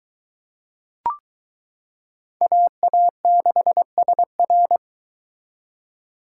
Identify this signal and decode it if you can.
Morse code, heard as a keyed tone, about 23 wpm, 700 Hz. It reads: AA6SR